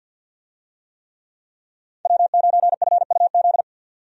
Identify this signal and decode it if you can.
Morse code, heard as a keyed tone, about 51 wpm, 700 Hz. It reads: W9FUZ
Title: USA Call Signs - 5 Characters - 50wpm